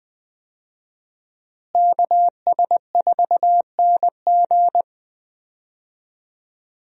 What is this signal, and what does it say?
Morse code, heard as a keyed tone, about 20 wpm, 700 Hz. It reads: KS4NG
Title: USA Call Signs - 5 Characters - Repeated 2x - 20wpm